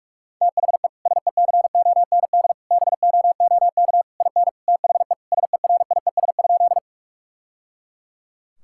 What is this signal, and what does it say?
Morse code, heard as a keyed tone, about 46 wpm, 700 Hz. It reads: THE SECOND BOOK IN THE SERIES?
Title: Sentences from Top 1000 Words - Review Part 5 - 45wpm